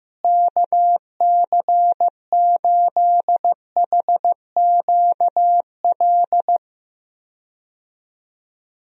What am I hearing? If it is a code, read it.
Morse code, heard as a keyed tone, about 15 wpm, 700 Hz. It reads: KC8HQL
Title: USA Call Signs - 6 Characters - Repeated 2x - 15wpm